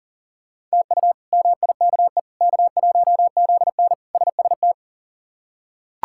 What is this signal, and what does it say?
Morse code, heard as a keyed tone, about 40 wpm, 700 Hz. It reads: TU MIKE K1ZN SST
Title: K1USN SST QSO Practice - 40wpm